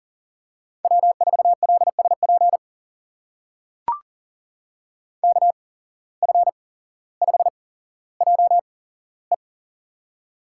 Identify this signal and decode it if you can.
Morse code, heard as a keyed tone, about 43 wpm, spaced as for 17 wpm, 700 Hz. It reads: W4LSP KF5JE
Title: USA Call Signs - 5 Characters - 8x Character Spacing - 40wpm